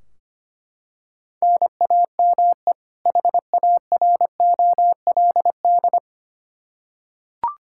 Morse code, heard as a keyed tone, about 25 wpm, 700 Hz. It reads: NAME HAROLD
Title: Practice QSO - Exchange 1 - Beginning - 25wpm